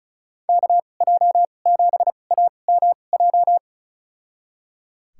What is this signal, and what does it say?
Morse code, heard as a keyed tone, about 34 wpm, spaced as for 25 wpm, 700 Hz. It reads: KJ7AMJ
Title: USA Call Signs - 2x Character Spacing - 35wpm